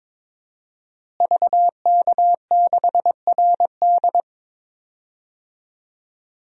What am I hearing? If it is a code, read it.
Morse code, heard as a keyed tone, about 22 wpm, 700 Hz. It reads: VK6RD
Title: DXCC-100 Call Signs - 5 Characters - Repeated 3x - 22wpm